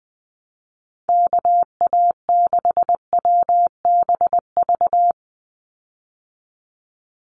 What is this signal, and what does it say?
Morse code, heard as a keyed tone, about 20 wpm, 700 Hz. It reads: KA6WBV
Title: USA Call Signs 20wpm